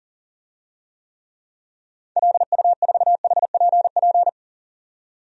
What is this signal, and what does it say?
Morse code, heard as a keyed tone, about 40 wpm, 700 Hz. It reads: LU4HPP
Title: DXCC-100 Call Signs - 6 Characters - 40wpm